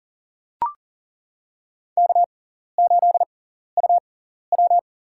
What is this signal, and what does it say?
Morse code, heard as a keyed tone, about 40 wpm, spaced as for 14 wpm, 700 Hz. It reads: K8UW
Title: USA Call Signs - 4 Characters - 6x Character Spacing - 40wpm